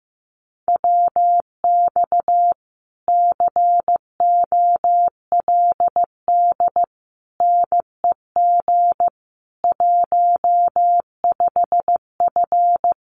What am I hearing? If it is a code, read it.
Morse code, heard as a keyed tone, about 15 wpm, 700 Hz. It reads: WX COLD NEG 15F
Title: Common QSO Phrases Encore 15wpm